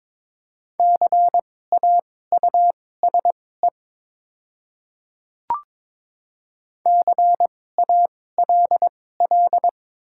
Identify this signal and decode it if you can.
Morse code, heard as a keyed tone, about 22 wpm, spaced as for 16 wpm, 700 Hz. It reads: CAUSE CALL
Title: 2x Character Spacing - Top 300 Words - 22wpm